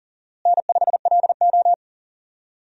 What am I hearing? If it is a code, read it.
Morse code, heard as a keyed tone, about 40 wpm, 700 Hz. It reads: N5LO
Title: USA Call Signs - 4 Characters - Repeated 2x - 40wpm